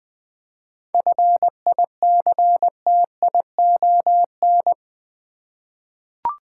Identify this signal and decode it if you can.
Morse code, heard as a keyed tone, about 20 wpm, 700 Hz. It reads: FICTION
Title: 7 Letter Words - First 3 Letters Given - 20wpm